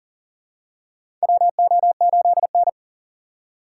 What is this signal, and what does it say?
Morse code, heard as a keyed tone, about 40 wpm, 700 Hz. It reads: WO8N
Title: USA Call Signs - 4 Characters - 6x Character Spacing - 40wpm